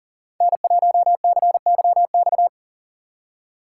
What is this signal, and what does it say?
Morse code, heard as a keyed tone, about 53 wpm, 700 Hz. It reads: N1CYX